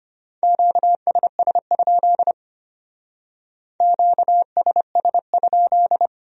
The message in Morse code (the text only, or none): QSS? QSS?